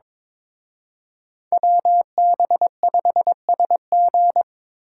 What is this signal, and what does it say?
Morse code, heard as a keyed tone, about 26 wpm, 700 Hz. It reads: WB5SG
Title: Speed Racing - USA 2x2 Call Signs 22wpm